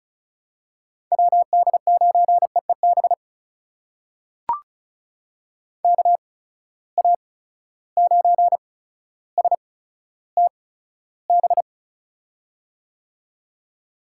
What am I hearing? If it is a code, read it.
Morse code, heard as a keyed tone, about 36 wpm, spaced as for 15 wpm, 700 Hz. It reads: WD9EEB KA9STB
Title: USA Call Signs - 6 Characters - 8x Character Spacing - 35wpm